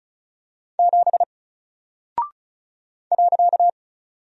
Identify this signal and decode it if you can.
Morse code, heard as a keyed tone, about 36 wpm, 700 Hz. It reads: 7 .
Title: Single Character (Letter:Number:Punctuation:Prosign) Alt 35wpm